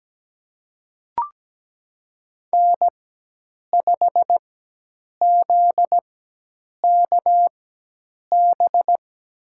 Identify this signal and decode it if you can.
Morse code, heard as a keyed tone, about 17 wpm, spaced as for 8 wpm, 700 Hz. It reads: N5ZKB
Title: USA Call Signs - 5 Characters - 4x Character Spacing - 17wpm